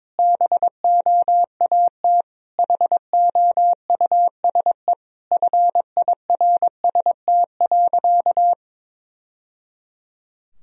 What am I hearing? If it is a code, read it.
Morse code, heard as a keyed tone, about 22 wpm, 700 Hz. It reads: BOAT HOUSE FIRST.
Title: Sentences from Top 1000 Words - Review Part 3 - 22wpm